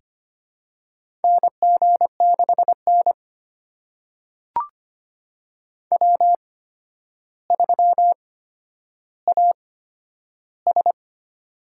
Morse code, heard as a keyed tone, about 25 wpm, spaced as for 11 wpm, 700 Hz. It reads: NG6N W3AS
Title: USA Call Signs - 4 Characters - 8x Character Spacing - 25wpm